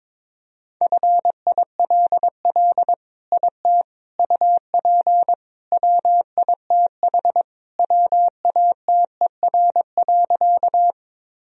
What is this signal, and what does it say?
Morse code, heard as a keyed tone, about 22 wpm, 700 Hz. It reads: FILL IT UP WITH WATER.
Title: Sentences from Top 1000 Words - Review Part 3 - 22wpm